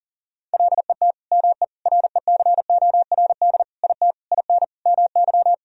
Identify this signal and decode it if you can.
Morse code, heard as a keyed tone, about 40 wpm, 700 Hz. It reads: LET ME RECORD IT IN MY
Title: Sentences from Top 1000 Words - Review Part 3 - 40wpm